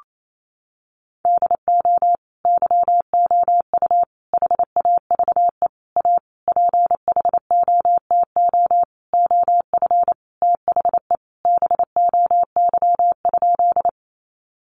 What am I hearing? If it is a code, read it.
Morse code, heard as a keyed tone, about 28 wpm, 700 Hz. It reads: DO YOU HAVE A PHOTO OF THE BOY?
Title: Sentences from Top 1200 Words 28wpm